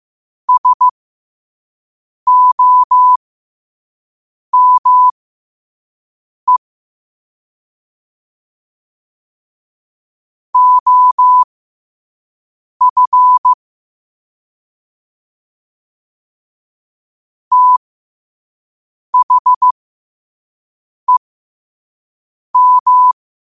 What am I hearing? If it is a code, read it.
Morse code, heard as a keyed tone, about 15 wpm, spaced as for 5 wpm, 1000 Hz. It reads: SOME OF THEM
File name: qotdfiles / 5wpm